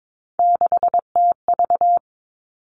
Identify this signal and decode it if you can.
Morse code, heard as a keyed tone, about 22 wpm, 700 Hz. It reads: 6TV
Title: Three Challenging Letter-Number Combinations 22wpm